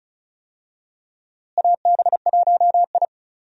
Morse code, heard as a keyed tone, about 36 wpm, 700 Hz. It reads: AB1I
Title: USA Call Signs - 4 Characters - Repeated 3x - 35wpm